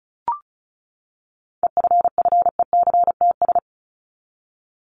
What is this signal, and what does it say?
Morse code, heard as a keyed tone, about 34 wpm, 700 Hz. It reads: EFFECTS
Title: ES:S Suffix - Top 100 Words with Base Words 35wpm